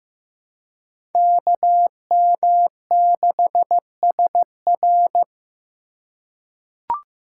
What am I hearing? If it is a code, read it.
Morse code, heard as a keyed tone, about 15 wpm, 700 Hz. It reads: KM6SR
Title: USA Call Signs - 5 Characters - 4x Character Spacing - 15wpm